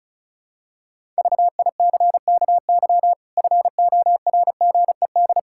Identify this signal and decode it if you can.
Morse code, heard as a keyed tone, about 36 wpm, 700 Hz. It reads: VICKY FORGED
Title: Pangram Challenge v2 - 35wpm